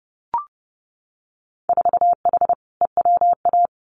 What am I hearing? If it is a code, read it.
Morse code, heard as a keyed tone, about 30 wpm, 700 Hz. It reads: 4H EWA
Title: Winter Field Day Exchanges Encore 30wpm